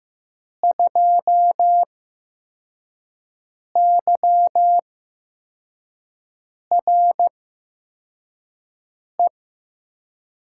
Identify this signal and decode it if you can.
Morse code, heard as a keyed tone, about 15 wpm, spaced as for 4 wpm, 700 Hz. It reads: 2YRE
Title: USA Call Signs - 6 Characters - 8x Character Spacing - 15wpm